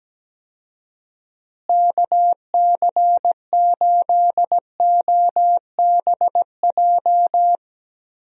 Morse code, heard as a keyed tone, about 17 wpm, 700 Hz. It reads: KC8OBJ
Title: USA Call Signs - 6 Characters - Repeated 2x - 17wpm